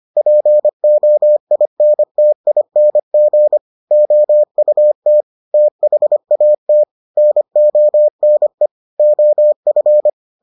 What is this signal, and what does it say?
Morse code, heard as a keyed tone, about 25 wpm, 600 Hz. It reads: POINTING OUT THAT NONE OF